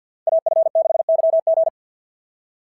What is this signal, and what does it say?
Morse code, heard as a keyed tone, about 46 wpm, 660 Hz. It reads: AF6YG
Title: SOTA QSO Practice 50wpm